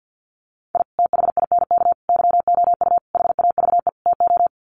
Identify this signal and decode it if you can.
Morse code, heard as a keyed tone, about 50 wpm, 700 Hz. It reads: I THINK YOU HAVE TO
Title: Sets of 5 Words 50wpm